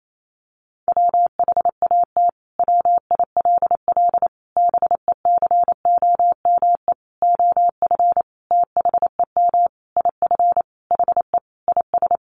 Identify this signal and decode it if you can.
Morse code, heard as a keyed tone, about 28 wpm, 700 Hz. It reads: WHAT WILL BECOME OF THEM IF HE IS
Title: Sentences from Top 1100 Words 28wpm